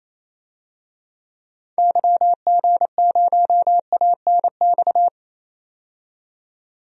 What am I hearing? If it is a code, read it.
Morse code, heard as a keyed tone, about 28 wpm, 700 Hz. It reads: YG0ANX
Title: DXCC-100 Call Signs - 6 Characters - 28wpm